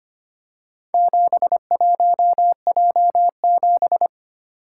Morse code, heard as a keyed tone, about 25 wpm, 700 Hz. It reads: 71J7
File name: practice / 25wpm / Four Letter-Number Combinations - 25wpm